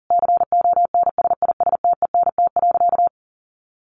Morse code, heard as a keyed tone, about 40 wpm, 700 Hz. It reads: CONSISTENT.
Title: Sentences from Top 1500 Words 40wpm